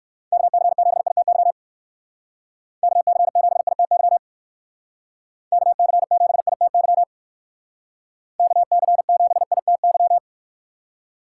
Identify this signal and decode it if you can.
Morse code, heard as a keyed tone, about 54 wpm, 700 Hz. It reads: KC7ITY KC7ITY KC7ITY KC7ITY
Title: Speed Racing - USA 2x3 Call Signs 45wpm